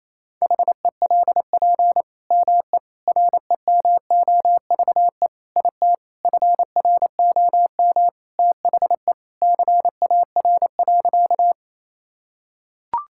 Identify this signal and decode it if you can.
Morse code, heard as a keyed tone, about 28 wpm, 700 Hz. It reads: HELP ME REMOVE IT FROM THE CAR.
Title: Sentences from Top 1000 Words - Review Part 5 - 28wpm